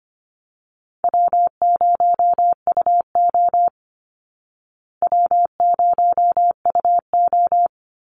Morse code, heard as a keyed tone, about 25 wpm, 700 Hz. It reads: W0UO W0UO
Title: Easy Call Signs Repeated 2x - 25wpm